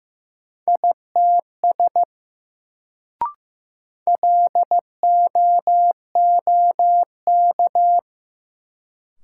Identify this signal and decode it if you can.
Morse code, heard as a keyed tone, about 15 wpm, 700 Hz. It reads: ITS LOOK
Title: Sets of 2 Words - New Word Drill - 15wpm